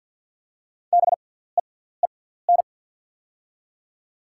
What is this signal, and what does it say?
Morse code, heard as a keyed tone, about 48 wpm, spaced as for 17 wpm, 700 Hz. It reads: BEEN
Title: 6x Character Spacing - Top 100 Words - 50wpm